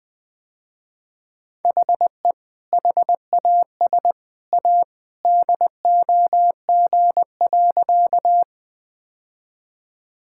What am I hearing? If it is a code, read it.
Morse code, heard as a keyed tone, about 20 wpm, 700 Hz. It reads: HE HAS A DOG.
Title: Sentences with 2-Word Phrases - Spoken Beginning - 20wpm